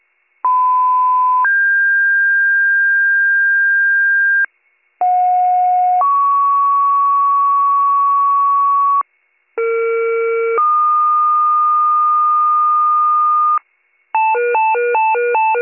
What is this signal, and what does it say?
Two-tone page, heard as a one-second tone, then a three-second tone, about 1010 then 1600 Hz, about 730 then 1120 Hz, about 460 then 1220 Hz. It is heard from 0.4 to 13.6 s.